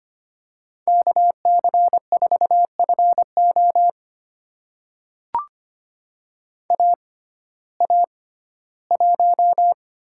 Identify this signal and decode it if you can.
Morse code, heard as a keyed tone, about 25 wpm, spaced as for 15 wpm, 700 Hz. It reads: KC4FO AA1
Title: USA Call Signs - 6x Character Spacing - 25wpm